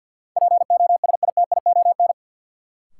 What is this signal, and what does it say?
Morse code, heard as a keyed tone, about 47 wpm, 700 Hz. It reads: POSITION
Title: Sentences from Top 900 Words - New Word Drill - 50wpm